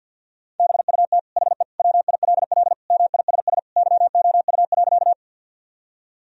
Morse code, heard as a keyed tone, about 50 wpm, 700 Hz. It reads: BUT HE WILL MISS YOU.
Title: Sentences from Top 1000 Words - Review Part 3 - 50wpm